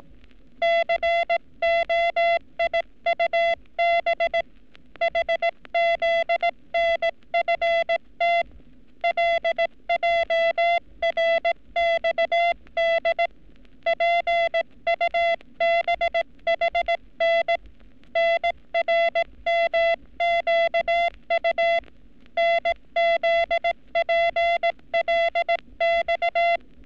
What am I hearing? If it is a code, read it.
Morse code, heard as a keyed tone, about 18 wpm, 680 Hz. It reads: COIUB HZNFT LJRXD PUBHN NRMQU NZPLX
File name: training / NAVPERS 11224-11235 / NAVPERS 11233 RA RB